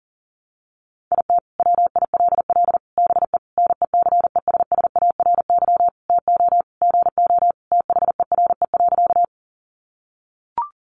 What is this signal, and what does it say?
Morse code, heard as a keyed tone, about 40 wpm, 700 Hz. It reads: IT WILL BE NECESSARY TO GO THERE.
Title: Sentences from Top 800 Words 40wpm